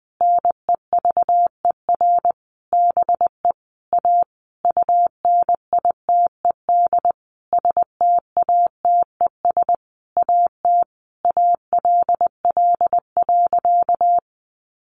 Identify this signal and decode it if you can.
Morse code, heard as a keyed tone, about 20 wpm, 700 Hz. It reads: NEVER BE A UNITED STATES AT ALL.
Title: A Sound Of Thunder 20wpm